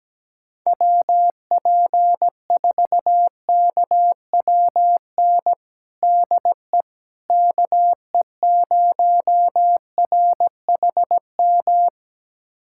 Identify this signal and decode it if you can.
Morse code, heard as a keyed tone, about 17 wpm, 700 Hz. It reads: WP4KWN DE KE0RHM
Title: USA Callsign DE Callsign - 6 Characters - 17wpm